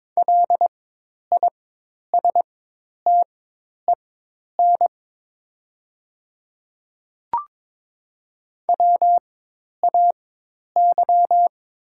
Morse code, heard as a keyed tone, about 22 wpm, spaced as for 10 wpm, 700 Hz. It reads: LISTEN WAY
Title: 4x Character Spacing - Top 500 Words - 22wpm